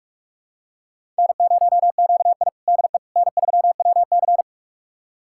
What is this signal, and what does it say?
Morse code, heard as a keyed tone, about 42 wpm, 700 Hz. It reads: N0QI DE N3WC
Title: USA Callsign DE Callsign - 4 Characters - 45wpm